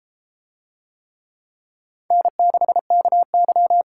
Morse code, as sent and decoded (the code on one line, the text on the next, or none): -. -.... -.- -.--
N6KY